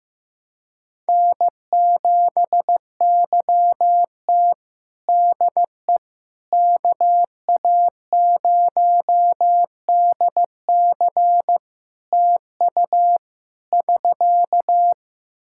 Morse code, heard as a keyed tone, about 15 wpm, 700 Hz. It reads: N7YT DE KA0DC TU <SK>
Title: Practice QSO - Exchange 4 - Ending - 15wpm